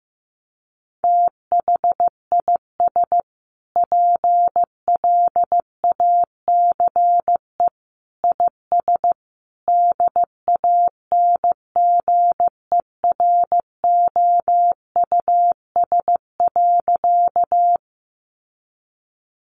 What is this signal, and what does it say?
Morse code, heard as a keyed tone, about 15 wpm, 700 Hz. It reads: THIS PLACE IS DANGEROUS.
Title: Sentences from Top 1200 Words 15wpm